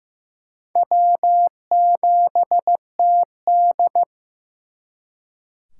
Morse code, heard as a keyed tone, about 18 wpm, 700 Hz. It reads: W7TD
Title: Speed Racing - USA Call Signs 15wpm